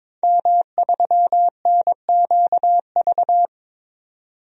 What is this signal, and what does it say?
Morse code, heard as a keyed tone, about 22 wpm, 700 Hz. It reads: M3NQV
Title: DXCC-100 Call Signs - 5 Characters - 22wpm